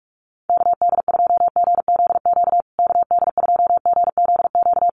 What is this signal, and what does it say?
Morse code, heard as a keyed tone, about 44 wpm, 700 Hz. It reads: KD2GZQ KD2GZQ
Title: USA Call Signs Repeated 3x 45wpm